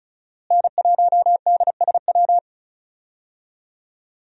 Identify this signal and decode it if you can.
Morse code, heard as a keyed tone, about 35 wpm, 700 Hz. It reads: N1DSW
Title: USA Call Signs - 5 Characters - Repeated 3x - 35wpm